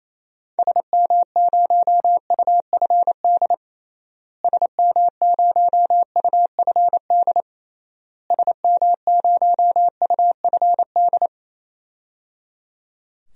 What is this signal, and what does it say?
Morse code, heard as a keyed tone, about 28 wpm, 700 Hz. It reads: SM0UFD SM0UFD SM0UFD
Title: DXCC-100 Call Signs - 6 Characters - Repeated 3x - 28wpm